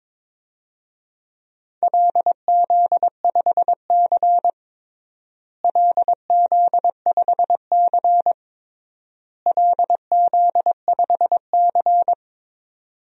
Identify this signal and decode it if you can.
Morse code, heard as a keyed tone, about 21 wpm, 700 Hz. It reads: LZ5C LZ5C LZ5C
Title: DXCC-100 Call Signs - 4 Characters - Repeated 3x - 22wpm